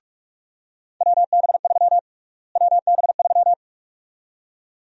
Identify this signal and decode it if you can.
Morse code, heard as a keyed tone, about 41 wpm, 700 Hz. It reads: WB3 WB3
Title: USA Call Sign Prefixes - Repeated 2x - 45wpm